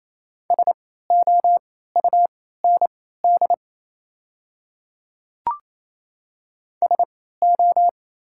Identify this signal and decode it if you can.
Morse code, heard as a keyed tone, about 28 wpm, spaced as for 16 wpm, 690 Hz. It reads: SOUND SO
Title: 3x Character Spacing - Top 100 Words - 28wpm